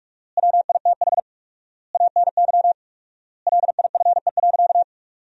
Morse code, heard as a keyed tone, about 46 wpm, 700 Hz. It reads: WITH ANY LIFE.
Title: 4x Word-Spacing - Sentences from Top 300 Words - 45wpm